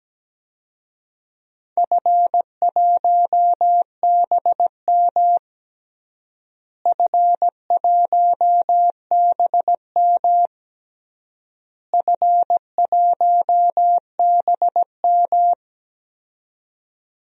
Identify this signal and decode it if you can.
Morse code, heard as a keyed tone, about 17 wpm, 700 Hz. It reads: F1BM F1BM F1BM
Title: DXCC-100 Call Signs - 4 Characters - Repeated 3x - 17wpm